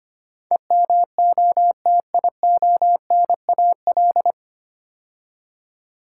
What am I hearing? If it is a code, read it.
Morse code, heard as a keyed tone, about 25 wpm, 700 Hz. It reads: EMOTIONAL
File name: practice / 25wpm / 9 Letter Words - First 3 Letters Given - 25wpm